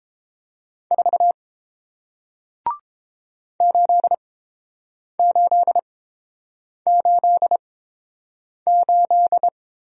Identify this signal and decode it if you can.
Morse code, heard as a keyed tone, about 30 wpm, spaced as for 7 wpm, 700 Hz. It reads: 4 8888